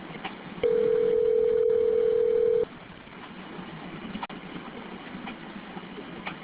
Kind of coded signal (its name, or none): telephone call progress tone